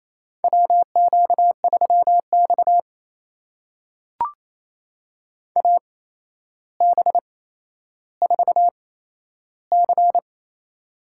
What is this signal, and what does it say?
Morse code, heard as a keyed tone, about 28 wpm, spaced as for 12 wpm, 700 Hz. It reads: WQ3X AB4C
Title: USA Call Signs - 8x Character Spacing - 28wpm